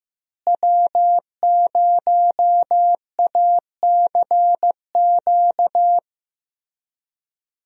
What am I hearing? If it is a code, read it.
Morse code, heard as a keyed tone, about 16 wpm, 700 Hz. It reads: W0ACQ
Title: USA Call Signs - 5 Characters - Repeated 3x - 15wpm